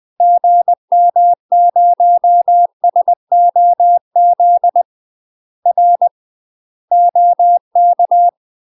Morse code, heard as a keyed tone, about 20 wpm, 700 Hz. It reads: GM0SOZ R OK
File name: UK MASTER QSOs TRAINING / 20WPM / UK MASTER QSOs 0055